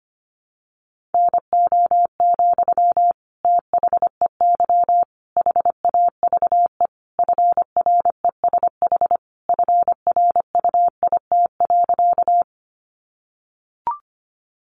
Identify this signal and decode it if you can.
Morse code, heard as a keyed tone, about 25 wpm, 700 Hz. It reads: NO, THEY HAVE FRESH FRUIT.
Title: Sentences from Top 1300 Words 25wpm